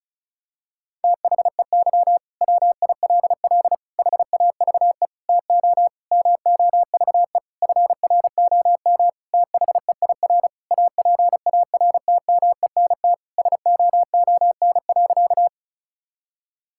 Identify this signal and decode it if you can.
Morse code, heard as a keyed tone, about 34 wpm, 700 Hz. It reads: THEY WILL HAVE TO MOVE FROM THEIR APARTMENT SOON.
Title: Sentences from Top 1000 Words - Review Part 6 - 35wpm